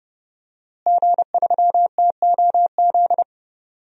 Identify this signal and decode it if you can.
Morse code, heard as a keyed tone, about 30 wpm, 700 Hz. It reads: G3TOZ
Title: DXCC-100 Call Signs - 5 Characters - Repeated 3x - 30wpm